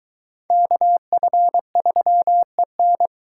Morse code, heard as a keyed tone, about 24 wpm, 700 Hz. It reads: KF3EN